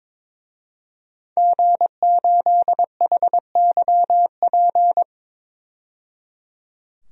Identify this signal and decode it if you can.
Morse code, heard as a keyed tone, about 22 wpm, 700 Hz. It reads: G8HYP
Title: DXCC-100 Call Signs - 5 Characters - 22wpm